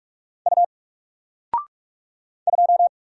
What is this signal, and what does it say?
Morse code, heard as a keyed tone, about 44 wpm, 700 Hz. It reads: U 2
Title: Single Letter-Number Alt 45wpm